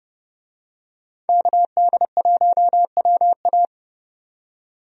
Morse code, heard as a keyed tone, about 30 wpm, 700 Hz. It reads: KD1WA
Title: USA Call Signs - 5 Characters - 30wpm